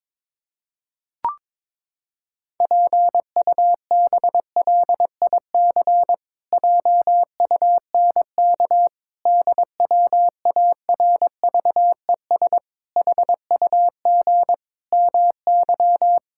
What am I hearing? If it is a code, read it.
Morse code, heard as a keyed tone, about 22 wpm, 700 Hz. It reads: PUBLIC JUNK DWARVES HUG MY
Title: Pangram Challenge v2 - 22wpm